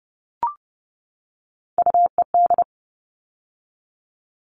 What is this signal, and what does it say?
Morse code, heard as a keyed tone, about 30 wpm, 700 Hz. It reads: UED
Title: All Cut Numbers - 3-Digit 30wpm